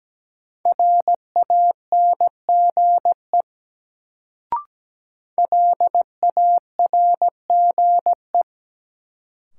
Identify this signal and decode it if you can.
Morse code, heard as a keyed tone, about 17 wpm, 700 Hz. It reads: RANGE LARGE
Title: Sets of 4 Words - New Word Drill - 17wpm